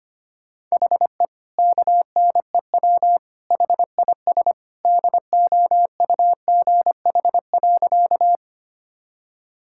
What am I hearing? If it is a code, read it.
Morse code, heard as a keyed tone, about 25 wpm, 700 Hz. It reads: HE KNEW HIS DOUGH.
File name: practice / 25wpm / All-Age Appropriate Jokes - 25wpm